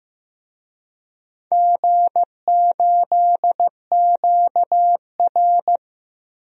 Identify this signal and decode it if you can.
Morse code, heard as a keyed tone, about 15 wpm, 700 Hz. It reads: G8QR
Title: DXCC-100 Call Signs - 4 Characters - 15wpm